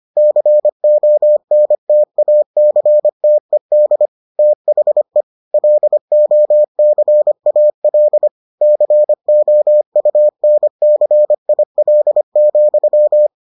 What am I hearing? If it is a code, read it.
Morse code, heard as a keyed tone, about 25 wpm, 600 Hz. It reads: CONTACTED THE LOCAL COUNCIL,